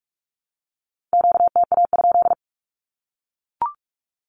Morse code, heard as a keyed tone, about 44 wpm, 700 Hz. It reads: QTA?
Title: Q-Codes 45wpm